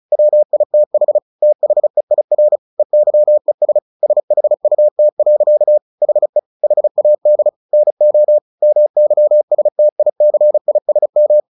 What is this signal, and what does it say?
Morse code, heard as a keyed tone, about 34 wpm, 600 Hz. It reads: WITH THEIR EYES SHUT. HE HAD NO MYSTICISM
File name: the old man and the sea / the old man and the sea 1 35wpm